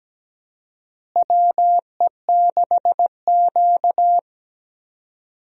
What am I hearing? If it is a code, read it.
Morse code, heard as a keyed tone, about 17 wpm, 700 Hz. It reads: WE6Q
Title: USA Call Signs - 4 Characters - Repeated 2x - 17wpm